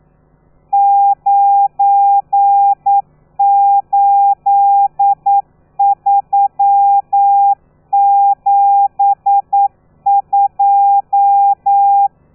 Morse code, heard as a keyed tone, about 9 wpm, 800 Hz. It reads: 98372
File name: telegrafia / i2tfj / Lez12 40